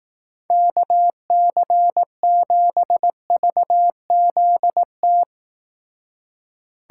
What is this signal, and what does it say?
Morse code, heard as a keyed tone, about 18 wpm, 700 Hz. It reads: KC7VZT
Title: Speed Racing - USA 2x3 Call Signs 15wpm